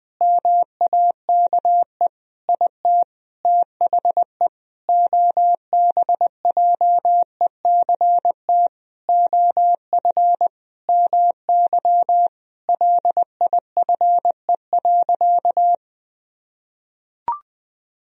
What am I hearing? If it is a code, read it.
Morse code, heard as a keyed tone, about 21 wpm, 700 Hz. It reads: MAKE IT THE OBJECT OF MY LIFE.
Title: Sentences from Top 1000 Words - Review Part 3 - 20wpm